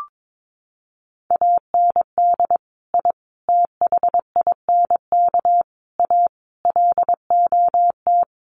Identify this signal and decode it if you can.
Morse code, heard as a keyed tone, about 22 wpm, 700 Hz. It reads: AND I THINK A LOT
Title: Sets of 5 Words Encore2 22wpm